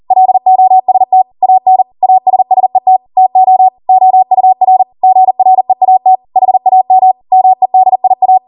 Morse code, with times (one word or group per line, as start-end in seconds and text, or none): 0.1-1.2 LOST
1.4-1.8 AN
2.0-3.0 ASSET
3.2-3.7 TO
3.9-4.8 OUR
5.0-6.2 GREAT
6.4-7.1 HAM
7.3-8.4 MEDIA